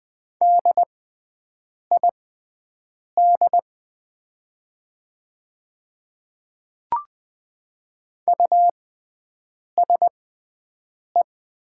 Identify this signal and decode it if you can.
Morse code, heard as a keyed tone, about 20 wpm, spaced as for 7 wpm, 700 Hz. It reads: DID USE
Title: 6x Character Spacing - Top 100 Words - 20wpm